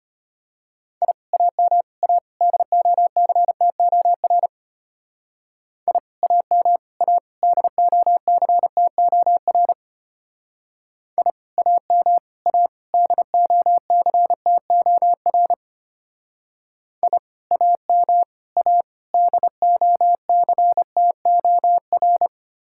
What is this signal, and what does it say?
Morse code, heard as a keyed tone, about 32 wpm, 700 Hz. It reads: I AM A DOCTOR I AM A DOCTOR I AM A DOCTOR I AM A DOCTOR